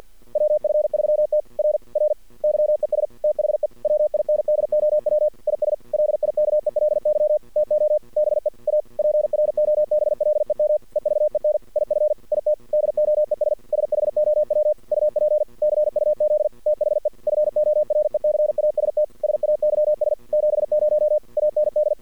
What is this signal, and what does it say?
Morse code, heard as a keyed tone, about 53 wpm, 600 Hz. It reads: WF3T R R YES THE WINDOW IS LIKELY TO BE A PROBLEM EVEN IF IT DOES SHOW UP CUZ THE FORECAST SAYS 70 AND
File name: archived / CRSnet